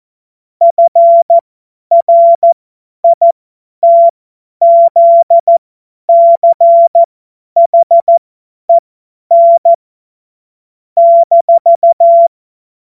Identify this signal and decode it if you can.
Morse code, heard as a keyed tone, about 14 wpm, spaced as for 10 wpm, 680 Hz. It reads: FRITZCHEN -
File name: Morsekurs / Witze / Witz 14. März 2023 50 F70